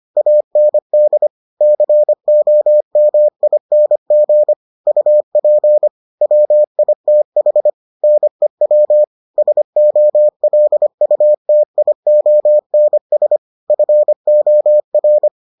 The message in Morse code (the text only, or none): AND COMING UP WITH NEW SOLUTIONS FOR